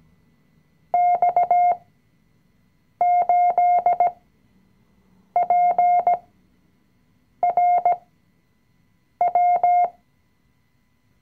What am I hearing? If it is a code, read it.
Morse code, heard as a keyed tone, about 17 wpm, spaced as for 6 wpm, 710 Hz. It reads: X8PRW
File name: CW-Online-Kurs / Anfänger / Lektion 020